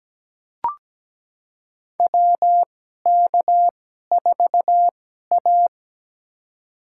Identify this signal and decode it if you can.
Morse code, heard as a keyed tone, about 17 wpm, spaced as for 12 wpm, 700 Hz. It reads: WK4A
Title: USA Call Signs - 4 Characters - 2x Character Spacing - 17wpm